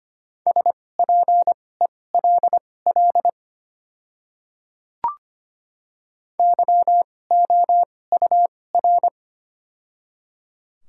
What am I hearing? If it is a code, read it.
Morse code, heard as a keyed tone, about 25 wpm, spaced as for 18 wpm, 690 Hz. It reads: SPELL YOUR
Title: 2x Character Spacing - Top 500 Words - 25wpm